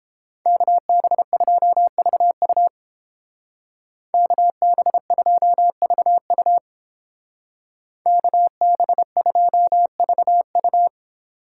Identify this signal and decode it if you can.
Morse code, heard as a keyed tone, about 31 wpm, 700 Hz. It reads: KB2VU KB2VU KB2VU